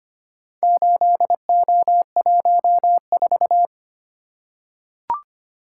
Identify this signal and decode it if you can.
Morse code, heard as a keyed tone, about 25 wpm, 700 Hz. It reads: 8O14
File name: practice / 25wpm / Four Letter-Number Combinations - 25wpm